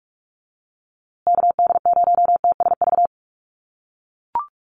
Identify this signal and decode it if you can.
Morse code, heard as a keyed tone, about 44 wpm, 700 Hz. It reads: KD0TSV
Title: USA Call Signs Repeated 2x 45wpm